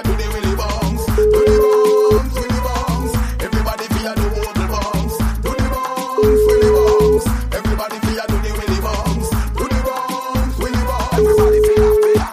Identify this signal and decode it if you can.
Ringback tone (Europe, 425 Hz), with noise.